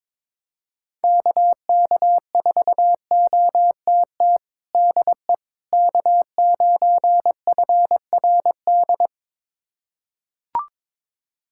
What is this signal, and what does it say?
Morse code, heard as a keyed tone, about 21 wpm, 700 Hz. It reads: KK4OTT DE K9FRD
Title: USA Callsign DE Callsign 22wpm